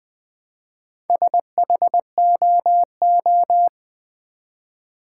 Morse code, heard as a keyed tone, about 20 wpm, 700 Hz. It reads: SHOO
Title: Build A Word - 1 Letter At A Time - 3 to 5-Letter Words - 20wpm